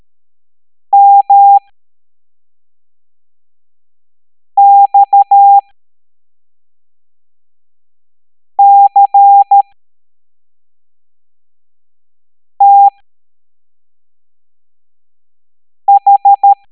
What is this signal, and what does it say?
Morse code, heard as a keyed tone, about 13 wpm, spaced as for 3 wpm, 800 Hz. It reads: MXCTH
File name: hamdata / L07-EX02-HQP